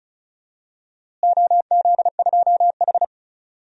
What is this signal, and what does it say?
Morse code, heard as a keyed tone, about 36 wpm, 700 Hz. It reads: OZ2H
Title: DXCC-100 Call Signs - 4 Characters - Repeated 3x - 35wpm